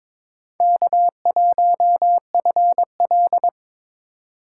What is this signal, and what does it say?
Morse code, heard as a keyed tone, about 22 wpm, 700 Hz. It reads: K1FL
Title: USA Call Signs - 3x Character Spacing - 22wpm